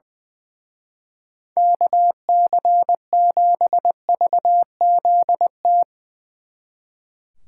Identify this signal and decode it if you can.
Morse code, heard as a keyed tone, about 21 wpm, 700 Hz. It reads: KC7VZT
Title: Speed Racing - USA 2x3 Call Signs 20wpm